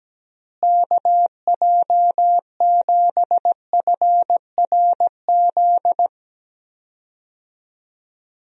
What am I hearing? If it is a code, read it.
Morse code, heard as a keyed tone, about 17 wpm, 700 Hz. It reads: KJ7FRZ
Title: USA Call Signs - 6 Characters - Repeated 3x - 17wpm